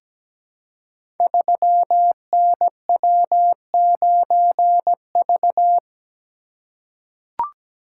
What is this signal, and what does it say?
Morse code, heard as a keyed tone, about 17 wpm, 700 Hz. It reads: 3NW9V
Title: Code Groups - 5 Letter:Numbers - 17wpm